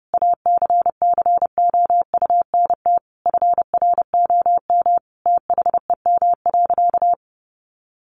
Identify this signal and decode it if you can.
Morse code, heard as a keyed tone, about 28 wpm, 700 Hz. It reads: ACCOUNT FROM THEM.
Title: Sentences from Top 900 Words 30wpm